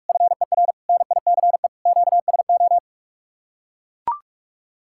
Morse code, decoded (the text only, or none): FER NICE QSO